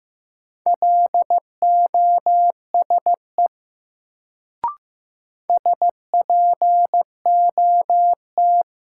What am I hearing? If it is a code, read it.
Morse code, heard as a keyed tone, about 15 wpm, 700 Hz. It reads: LOSE SPOT
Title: Sentences from Top 1100 Words - New Word Drill Alt - 15wpm